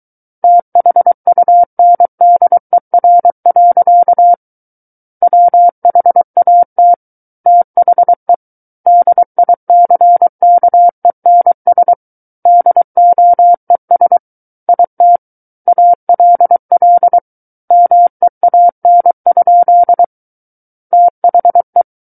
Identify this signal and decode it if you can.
Morse code, heard as a keyed tone, about 23 wpm, 700 Hz. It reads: THUNDER. WHAT THE DICKENS DOES IT ALL MEAN? THE
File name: Book One / 14-Chapter 14 - 23 WPM